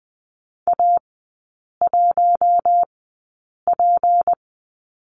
Morse code, heard as a keyed tone, about 19 wpm, spaced as for 8 wpm, 700 Hz. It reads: A1P
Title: 212 Lesson 34e - Number 8 - Three Characters 20wpm